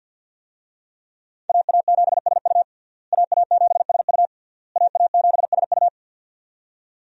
Morse code, heard as a keyed tone, about 51 wpm, 700 Hz. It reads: AA7SU AA7SU AA7SU
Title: USA Call Signs - 5 Characters - Repeated 3x - 50wpm